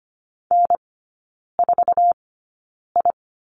Morse code, heard as a keyed tone, about 25 wpm, spaced as for 9 wpm, 700 Hz. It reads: N4I
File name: course / 25:10wpm / 093 Lesson 16e - Letter c - Three Characters 25wpm